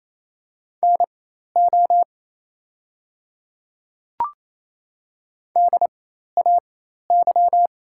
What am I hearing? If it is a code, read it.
Morse code, heard as a keyed tone, about 27 wpm, spaced as for 13 wpm, 690 Hz. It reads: NO DAY
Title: 4x Character Spacing - Top 200 Words - 28wpm